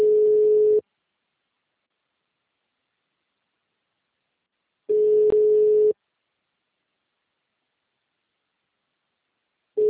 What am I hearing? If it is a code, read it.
Ringback tone (Europe, 425 Hz).
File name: out- -2015- - - .32340